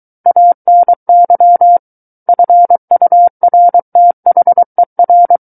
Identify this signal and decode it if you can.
Morse code, heard as a keyed tone, about 23 wpm, 700 Hz. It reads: ANY FURTHER
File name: Book One / 14-Chapter 14 - 23 WPM